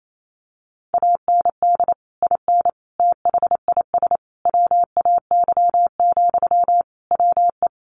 Morse code, heard as a keyed tone, about 28 wpm, 700 Hz. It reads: AND IN THIS WAY, WE
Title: Sentences from Top 1400 Words 28wpm